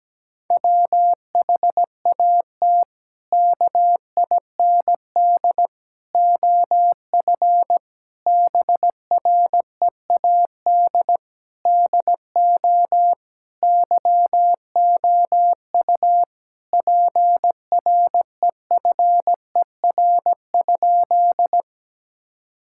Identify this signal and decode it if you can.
Morse code, heard as a keyed tone, about 17 wpm, 700 Hz. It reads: WHAT KIND OF BREAD DO YOU PREFER?
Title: Sentences from Top 1600 Words 17wpm